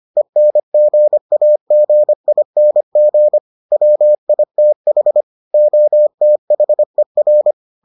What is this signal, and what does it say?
Morse code, heard as a keyed tone, about 25 wpm, 600 Hz. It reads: ENGAGING WITH OTHER